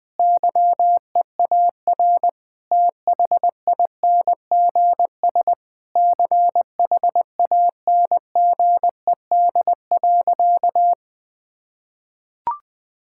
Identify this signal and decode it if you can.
Morse code, heard as a keyed tone, about 20 wpm, 700 Hz. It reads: YEAR THINGS CHANGED.
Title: Sentences with 5-Word Phrases - 20wpm